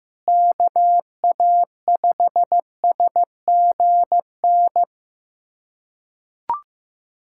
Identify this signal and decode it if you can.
Morse code, heard as a keyed tone, about 15 wpm, 700 Hz. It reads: KA5SGN
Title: USA Callsign DE Callsign - 6 Characters - 15wpm